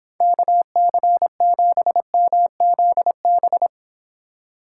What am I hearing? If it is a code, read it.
Morse code, heard as a keyed tone, about 26 wpm, 700 Hz. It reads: KC7MZB